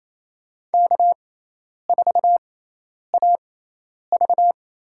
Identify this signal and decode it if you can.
Morse code, heard as a keyed tone, about 28 wpm, spaced as for 10 wpm, 700 Hz. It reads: K4AV
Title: USA Call Signs - 6x Character Spacing - 28wpm